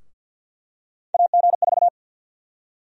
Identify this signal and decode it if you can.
Morse code, heard as a keyed tone, about 53 wpm, 700 Hz. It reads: AG4
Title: USA Call Sign Prefixes - 50wpm